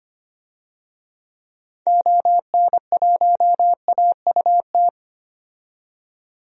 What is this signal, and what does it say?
Morse code, heard as a keyed tone, about 25 wpm, 700 Hz. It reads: ON1AUT
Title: DXCC-100 Call Signs - 6 Characters - 25wpm